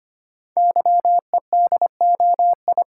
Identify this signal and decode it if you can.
Morse code, heard as a keyed tone, about 25 wpm, 700 Hz. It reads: YEDOI
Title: Code Groups - 5 Letters - 25wpm